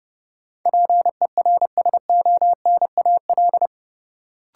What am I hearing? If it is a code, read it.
Morse code, heard as a keyed tone, about 32 wpm, 700 Hz. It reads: PERSONAL
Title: Sentences from Top 700 Words - New Word Drill - 30wpm